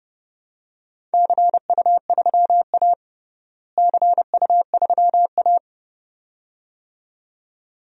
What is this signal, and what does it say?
Morse code, heard as a keyed tone, about 30 wpm, 700 Hz. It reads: CU3A CU3A
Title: DXCC-100 Call Signs - 4 Characters - Repeated 2x - 30wpm